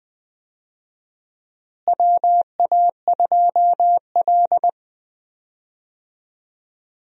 Morse code, heard as a keyed tone, about 20 wpm, 700 Hz. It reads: WA2L
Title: USA Call Signs - 4 Characters - Repeated 2x - 20wpm